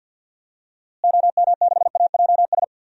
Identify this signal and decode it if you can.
Morse code, heard as a keyed tone, about 51 wpm, 700 Hz. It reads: OM6AJS
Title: DXCC-100 Call Signs - 6 Characters - Repeated 2x - 50wpm